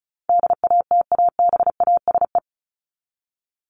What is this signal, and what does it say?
Morse code, heard as a keyed tone, about 34 wpm, 700 Hz. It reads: DATABASE
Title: Top 1751-2000 Words 35wpm